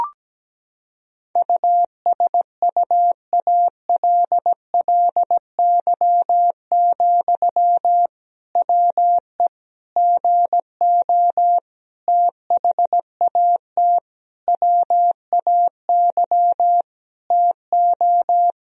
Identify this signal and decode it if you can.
Morse code, heard as a keyed tone, about 17 wpm, 700 Hz. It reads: USUALLY, WE GO THAT WAY TO
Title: Sentences from Top 1000 Words - Review Part 3 - 17wpm